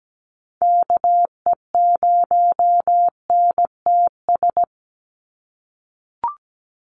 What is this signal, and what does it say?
Morse code, heard as a keyed tone, about 17 wpm, 700 Hz. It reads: KE0NTS
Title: USA Call Signs 17wpm